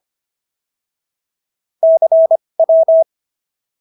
Morse code, heard as a keyed tone, about 25 wpm, spaced as for 20 wpm, 650 Hz. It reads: CW